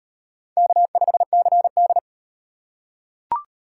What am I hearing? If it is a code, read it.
Morse code, heard as a keyed tone, about 38 wpm, 700 Hz. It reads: K5CD